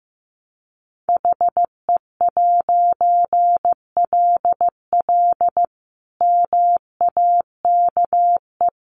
Morse code, heard as a keyed tone, about 15 wpm, 700 Hz. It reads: HE'LL MAKE
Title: A Sound Of Thunder 15wpm